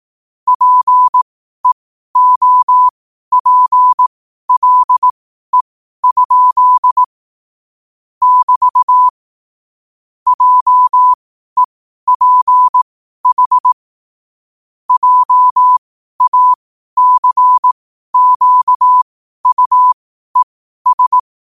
Morse code, heard as a keyed tone, about 18 wpm, spaced as for 12 wpm, 1000 Hz. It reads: PEOPLE? <BT> JEPH JACQUES
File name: qotdfiles / 12wpm